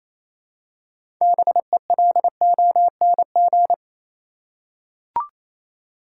Morse code, heard as a keyed tone, about 28 wpm, 700 Hz. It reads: BELONG
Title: Six Letter Words - 28wpm